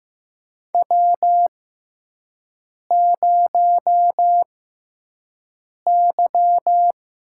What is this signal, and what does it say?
Morse code, heard as a keyed tone, about 15 wpm, spaced as for 5 wpm, 700 Hz. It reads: W0Y
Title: USA Call Signs - 6x Character Spacing - 15wpm